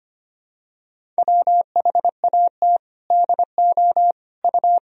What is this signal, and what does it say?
Morse code, heard as a keyed tone, about 25 wpm, 700 Hz. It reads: WHAT DO U
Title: Sentences from Top 100 Words and Abbreviations 25wpm